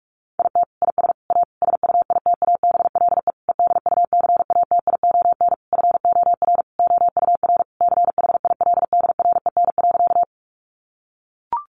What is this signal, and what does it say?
Morse code, heard as a keyed tone, about 44 wpm, 700 Hz. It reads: IT IS A SUITABLE EDUCATION FOR OUR CHILDREN.
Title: Sentences from Top 1400 Words 45wpm